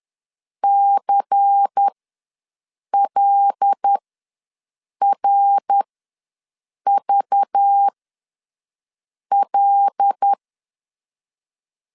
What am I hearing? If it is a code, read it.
Morse code, heard as a keyed tone, about 10 wpm, spaced as for 6 wpm, 800 Hz. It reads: CLRVL
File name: LEZIONI / 5 LBVC / 5 LBVC ws 10wpm 3733